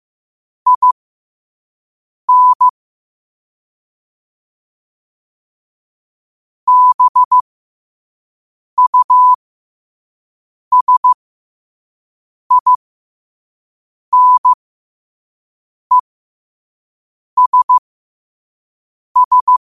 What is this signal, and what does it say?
Morse code, heard as a keyed tone, about 15 wpm, spaced as for 5 wpm, 1000 Hz. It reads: IN BUSINESS
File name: qotdfiles / 5wpm